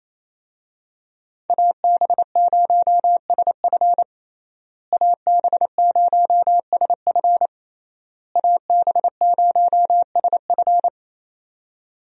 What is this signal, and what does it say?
Morse code, heard as a keyed tone, about 27 wpm, 700 Hz. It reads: AB0SF AB0SF AB0SF
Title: USA Call Signs - 5 Characters - Repeated 3x - 28wpm